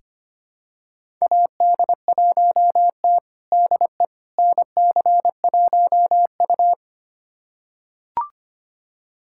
Morse code, heard as a keyed tone, about 25 wpm, 700 Hz. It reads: AD1T DE NC1U
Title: USA Callsign DE Callsign - 4 Characters - 25wpm